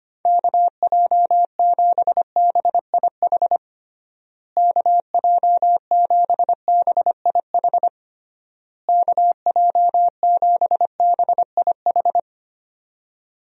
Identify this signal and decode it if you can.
Morse code, heard as a keyed tone, about 25 wpm, 700 Hz. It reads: KJ7BIH KJ7BIH KJ7BIH
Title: USA Call Signs - 6 Characters - Repeated 3x - 25wpm